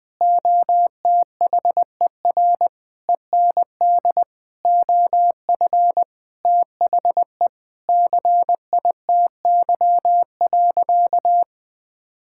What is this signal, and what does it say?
Morse code, heard as a keyed tone, about 20 wpm, 700 Hz. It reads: OTHER END OF THE CITY.
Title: Sentences with 5-Word Phrases - 20wpm